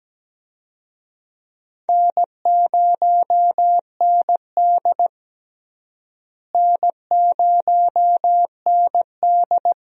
Morse code, heard as a keyed tone, about 17 wpm, 700 Hz. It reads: N0ND N0ND
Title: USA Call Signs - 4 Characters - Repeated 3x - 17wpm